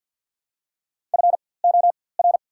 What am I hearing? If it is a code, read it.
Morse code, heard as a keyed tone, about 54 wpm, spaced as for 24 wpm, 690 Hz. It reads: FOR
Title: 4x Character Spacing - Top 100 Words - 50wpm